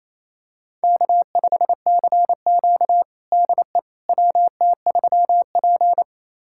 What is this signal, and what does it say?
Morse code, heard as a keyed tone, about 28 wpm, 700 Hz. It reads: K5CQ DE WT3P
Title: USA Callsign DE Callsign - 4 Characters - 28wpm